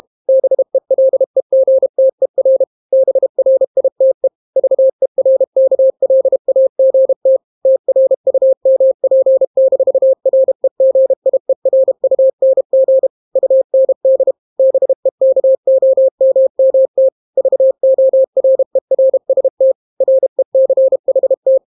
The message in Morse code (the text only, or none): BELEGTER BRITE VERKLAGT TRUMP-REGIERUNG UND BEKOMMT VORERST RECHT